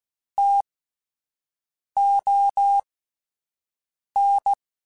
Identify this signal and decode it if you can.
Morse code, heard as a keyed tone, about 16 wpm, spaced as for 5 wpm, 800 Hz. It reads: TON
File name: Spaced Alphas / Spaced Alphas mp3s / Ts sp @10